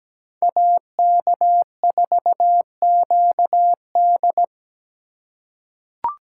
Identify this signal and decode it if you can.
Morse code, heard as a keyed tone, about 17 wpm, 700 Hz. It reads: AK4QD
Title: USA Call Signs - 5 Characters - 2x Character Spacing - 17wpm